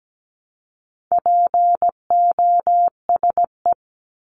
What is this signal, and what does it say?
Morse code, heard as a keyed tone, about 17 wpm, 700 Hz. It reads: POSE
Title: Top 500 Words Repeated 17wpm